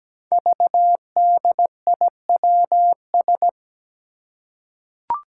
Morse code, heard as a keyed tone, about 17 wpm, 700 Hz. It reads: VDIWS
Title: Five Challenging Letter-Number Combinations 17wpm